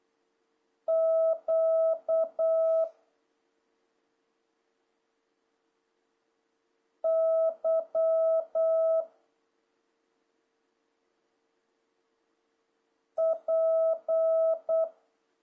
Morse code, heard as a keyed tone, about 8 wpm, 650 Hz. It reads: Q Y P
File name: jh8cbh / 2026cw